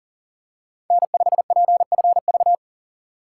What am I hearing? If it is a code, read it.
Morse code, heard as a keyed tone, about 40 wpm, 700 Hz. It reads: N5PFV
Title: USA Call Signs - 5 Characters - 6x Character Spacing - 40wpm